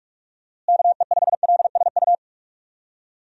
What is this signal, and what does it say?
Morse code, heard as a keyed tone, about 46 wpm, 700 Hz. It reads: KE5LSU
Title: USA Call Signs - 6 Characters - Repeated 2x - 45wpm